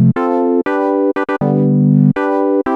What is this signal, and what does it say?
There is no code.